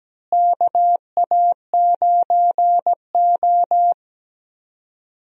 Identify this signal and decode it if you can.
Morse code, heard as a keyed tone, about 17 wpm, 700 Hz. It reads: KA9O
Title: USA Call Signs - 4 Characters - Repeated 2x - 17wpm